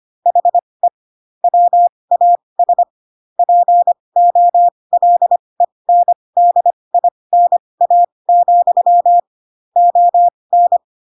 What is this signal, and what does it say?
Morse code, heard as a keyed tone, about 25 wpm, spaced as for 20 wpm, 700 Hz. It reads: HE WAS POLENDINA, ON
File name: Pinocchio / Pinocchio ch2 15 25 ik1hge